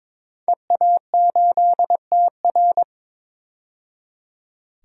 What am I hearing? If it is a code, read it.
Morse code, heard as a keyed tone, about 22 wpm, 700 Hz. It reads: EA8TR
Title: DXCC-100 Call Signs - 5 Characters - 22wpm